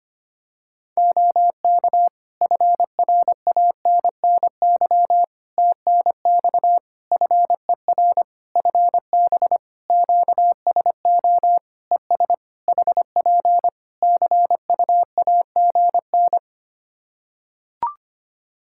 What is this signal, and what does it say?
Morse code, heard as a keyed tone, about 25 wpm, 700 Hz. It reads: OK FRANNY TNX FER FB QSO ES HP CUAGN
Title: Practice QSO - Exchange 4 - Ending - 25wpm